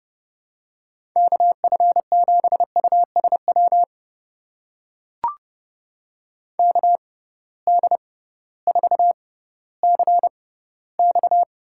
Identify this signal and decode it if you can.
Morse code, heard as a keyed tone, about 30 wpm, spaced as for 16 wpm, 700 Hz. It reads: KF7USW KD4CX
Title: USA Call Signs - 6 Characters - 6x Character Spacing - 30wpm